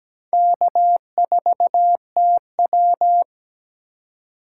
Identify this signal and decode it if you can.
Morse code, heard as a keyed tone, about 20 wpm, 700 Hz. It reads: K4TW
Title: Speed Racing - USA 1x2 Call Signs 17wpm